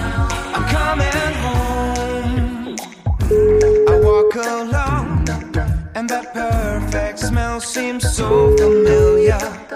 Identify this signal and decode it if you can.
Ringback tone (Europe, 425 Hz), with noise.